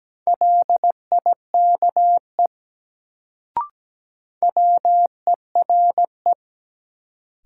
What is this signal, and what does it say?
Morse code, heard as a keyed tone, about 17 wpm, 700 Hz. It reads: LIKE WERE
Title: Sentences from Top 100 Words - New Word Drill - 17wpm